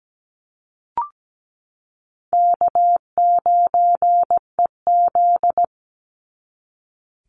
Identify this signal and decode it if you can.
Morse code, heard as a keyed tone, about 17 wpm, 700 Hz. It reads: K9EZ
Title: NA SOTA Chaser Callsigns 17wpm